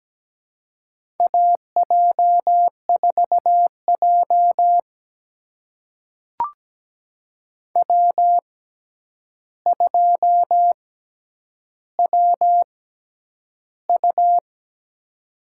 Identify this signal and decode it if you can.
Morse code, heard as a keyed tone, about 17 wpm, spaced as for 9 wpm, 700 Hz. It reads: AJ4J W2WU
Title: USA Call Signs - 4 Characters - 6x Character Spacing - 17wpm